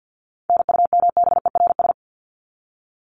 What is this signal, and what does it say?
Morse code, heard as a keyed tone, about 50 wpm, 700 Hz. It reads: NUMBERS
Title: ES:S Suffix - Top 100 Words with Base Words 50wpm